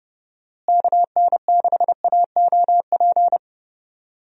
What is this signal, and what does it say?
Morse code, heard as a keyed tone, about 30 wpm, 700 Hz. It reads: KN6AOP
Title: USA Call Signs - 6 Characters - Repeated 2x - 30wpm